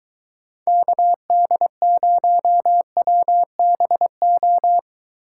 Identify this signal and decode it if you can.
Morse code, heard as a keyed tone, about 22 wpm, 700 Hz. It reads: KD0WBO